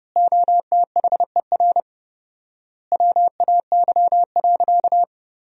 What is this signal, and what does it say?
Morse code, heard as a keyed tone, about 32 wpm, 700 Hz. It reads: OTHER WAY.
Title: 4x Word-Spacing - Sentences from Top 100 Words - 30wpm